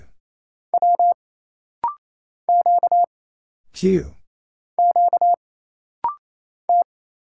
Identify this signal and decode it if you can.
Morse code, heard as a keyed tone, about 29 wpm, 700 Hz. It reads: W Q Q T
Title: Single Letters - Rapid-Fire 28wpm